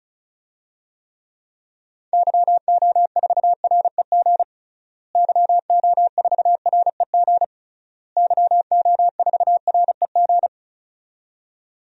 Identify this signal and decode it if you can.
Morse code, heard as a keyed tone, about 36 wpm, 700 Hz. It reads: YO4REG YO4REG YO4REG
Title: DXCC-100 Call Signs - 6 Characters - Repeated 3x - 35wpm